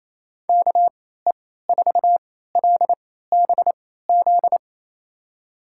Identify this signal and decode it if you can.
Morse code, heard as a keyed tone, about 28 wpm, spaced as for 16 wpm, 700 Hz. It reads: KE4LBZ
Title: USA Call Signs - 6 Characters - 3x Character Spacing - 28wpm